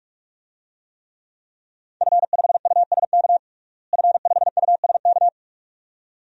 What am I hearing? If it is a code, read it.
Morse code, heard as a keyed tone, about 46 wpm, 700 Hz. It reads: F5USK F5USK
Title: DXCC-100 Call Signs - 5 Characters - Repeated 2x - 45wpm